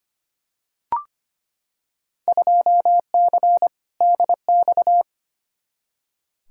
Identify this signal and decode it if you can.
Morse code, heard as a keyed tone, about 25 wpm, 700 Hz. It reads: 2C DX
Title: Field Day Exchanges Encore - 25wpm